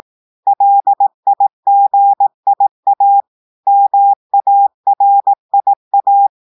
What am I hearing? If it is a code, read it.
Morse code, heard as a keyed tone, about 18 wpm, 800 Hz. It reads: LIGIA MARIA
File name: audios / portugais / portugais texte en clair 1 vitesse 18 mots minutes